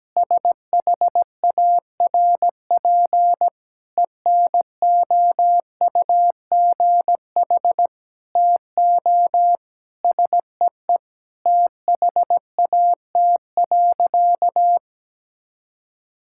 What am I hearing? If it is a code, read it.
Morse code, heard as a keyed tone, about 17 wpm, 700 Hz. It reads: SHARP ENOUGH TO SEE THAT.
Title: Sentences from Top 1600 Words 17wpm